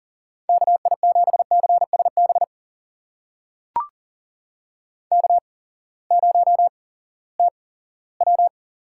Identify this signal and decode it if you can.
Morse code, heard as a keyed tone, about 40 wpm, spaced as for 20 wpm, 700 Hz. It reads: KI7CSB K0TW
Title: USA Call Signs - 8x Character Spacing - 40wpm